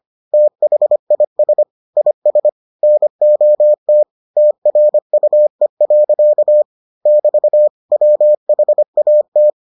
Morse code, heard as a keyed tone, about 25 wpm, 600 Hz. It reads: THIS IS NOT TRUE. <BT> WHAT